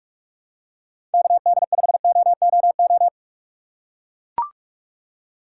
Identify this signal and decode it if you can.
Morse code, heard as a keyed tone, about 44 wpm, 700 Hz. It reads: KD5OOO
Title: USA Call Signs - 6 Characters - 4x Character Spacing - 45wpm